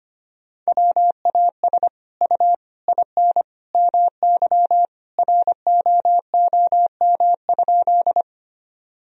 Morse code, heard as a keyed tone, about 25 wpm, 700 Hz. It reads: WAS U IN MY ROOM?
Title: Sentences from Top 300 Words and Abbreviations 25wpm